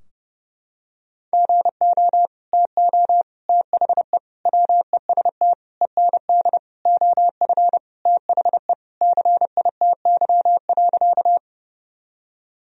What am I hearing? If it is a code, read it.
Morse code, heard as a keyed tone, about 32 wpm, 700 Hz. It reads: GO TO THE WEST END OF THE CITY.
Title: Sentences from Top 1000 Words - Review Part 2 - 30wpm